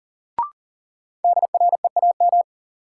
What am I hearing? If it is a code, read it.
Morse code, heard as a keyed tone, about 40 wpm, 700 Hz. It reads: DREAM
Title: Sentences from Top 1500 Words - New Word Drill - 40wpm